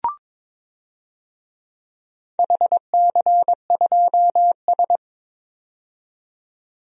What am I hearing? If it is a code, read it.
Morse code, heard as a keyed tone, about 23 wpm, 700 Hz. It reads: HC2S
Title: DXCC-100 Call Signs - 4 Characters - 22wpm